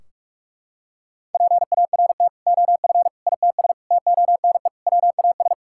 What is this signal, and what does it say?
Morse code, heard as a keyed tone, about 41 wpm, 700 Hz. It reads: PART OF ITS TONE WAS
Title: Sentences from Top 1600 Words 45wpm